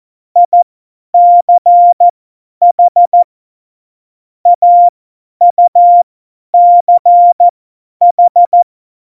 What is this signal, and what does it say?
Morse code, heard as a keyed tone, about 14 wpm, spaced as for 10 wpm, 700 Hz. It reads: ICH AUCH
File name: Morsekurs / Witze / Witz 12. Dezember 2017, 50 F70